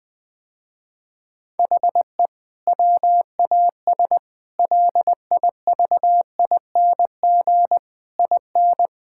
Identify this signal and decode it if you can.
Morse code, heard as a keyed tone, about 20 wpm, 700 Hz. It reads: HE WAS LIVING IN
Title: Sentences with 6-Word Phrases - Spoken Beginning - 20wpm